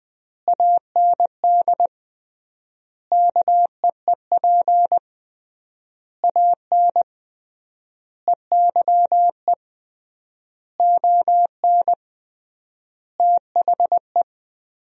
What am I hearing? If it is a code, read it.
Morse code, heard as a keyed tone, about 20 wpm, 700 Hz. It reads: AND KEEP AN EYE ON THE
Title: 3x Word-Spacing - Sentences from Top 300 Words - 20wpm